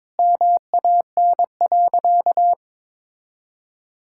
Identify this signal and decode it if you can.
Morse code, heard as a keyed tone, about 22 wpm, 700 Hz. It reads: MAN.
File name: practice / 22wpm / Sentences with 3-Word Phrases - 22wpm